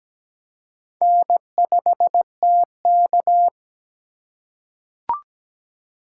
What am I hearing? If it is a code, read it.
Morse code, heard as a keyed tone, about 17 wpm, 700 Hz. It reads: N5TK
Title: USA Call Signs - 4 Characters - 2x Character Spacing - 17wpm